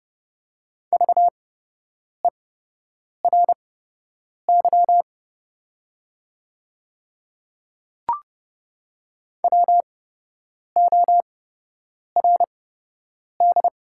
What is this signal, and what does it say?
Morse code, heard as a keyed tone, about 30 wpm, spaced as for 8 wpm, 700 Hz. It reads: VERY WORD
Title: 8x Character Spacing - Top 300 Words - 30wpm